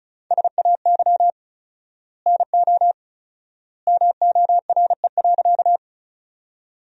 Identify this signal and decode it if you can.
Morse code, heard as a keyed tone, about 36 wpm, 700 Hz. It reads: SAY NO MORE.
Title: 4x Word-Spacing - Sentences from Top 200 Words - 35wpm